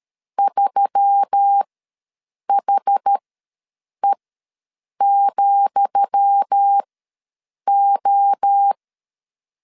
Morse code, heard as a keyed tone, about 13 wpm, spaced as for 7 wpm, 800 Hz. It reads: 3HE,O
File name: LEZIONI / 10 SIGNS / 10D COMA ws 12wpm 3618